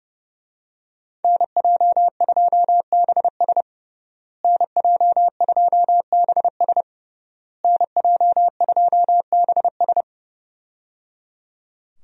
Morse code, heard as a keyed tone, about 30 wpm, 700 Hz. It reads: NJ2BS NJ2BS NJ2BS
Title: USA Call Signs - 5 Characters - Repeated 3x - 30wpm